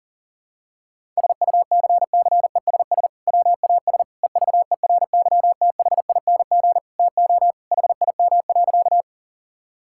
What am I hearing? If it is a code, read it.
Morse code, heard as a keyed tone, about 40 wpm, 700 Hz. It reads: SUCCESS WAS EVERYTHING TO HIM.
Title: Sentences from Top 1000 Words - Review Part 6 - 40wpm